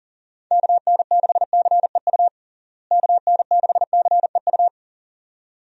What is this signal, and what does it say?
Morse code, heard as a keyed tone, about 40 wpm, 700 Hz. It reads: KN6CEU KN6CEU
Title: USA Call Signs - 6 Characters - Repeated 2x - 40wpm